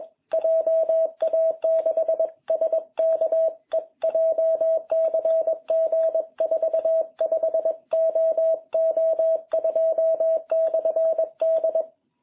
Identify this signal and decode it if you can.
Morse code, heard as a keyed tone, about 21 wpm, 660 Hz. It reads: JA6SKE JCG 45OO2/D